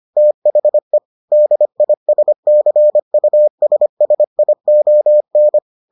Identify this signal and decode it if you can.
Morse code, heard as a keyed tone, about 25 wpm, 600 Hz. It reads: THE DISCUSSION